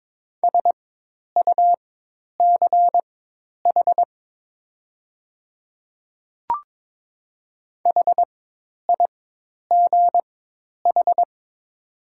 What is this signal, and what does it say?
Morse code, heard as a keyed tone, about 22 wpm, spaced as for 10 wpm, 700 Hz. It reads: SUCH HIGH
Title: 4x Character Spacing - Top 200 Words - 22wpm